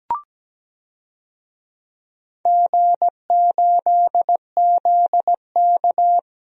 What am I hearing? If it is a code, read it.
Morse code, heard as a keyed tone, about 17 wpm, 700 Hz. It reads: G8ZK
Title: DXCC-100 Call Signs - 4 Characters - Repeated 2x - 17wpm